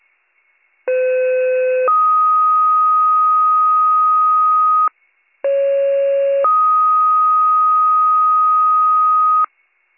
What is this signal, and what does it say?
Two-tone page, heard as a one-second tone, then a three-second tone, about 510 then 1250 Hz, about 550 then 1250 Hz. It is heard from 0.9 to 9.4 s.